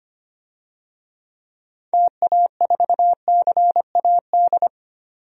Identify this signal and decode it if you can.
Morse code, heard as a keyed tone, about 25 wpm, 700 Hz. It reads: TA4CAD
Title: DXCC-100 Call Signs - 6 Characters - 25wpm